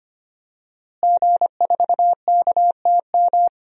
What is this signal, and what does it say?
Morse code, heard as a keyed tone, about 25 wpm, 700 Hz. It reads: G4KTM
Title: DXCC-100 Call Signs - 5 Characters - Repeated 3x - 25wpm